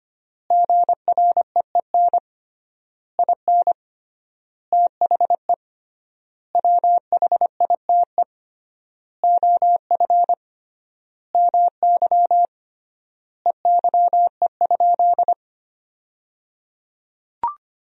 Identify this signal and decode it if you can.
Morse code, heard as a keyed tone, about 25 wpm, 700 Hz. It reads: GREEN IN THE WHITE OF MY EYE?
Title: 3x Word-Spacing - Sentences from Top 500 Words - 25wpm